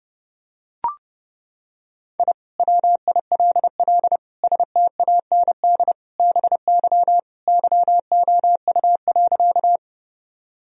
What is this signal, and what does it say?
Morse code, heard as a keyed tone, about 30 wpm, 700 Hz. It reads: I WILL STAND BY YOU.
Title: Sentences from Top 1000 Words - Review Part 1 - 30wpm